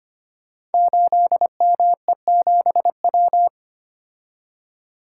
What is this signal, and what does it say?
Morse code, heard as a keyed tone, about 25 wpm, 700 Hz. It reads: 8ME7W
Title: Code Groups - 5 Letter:Numbers - 25wpm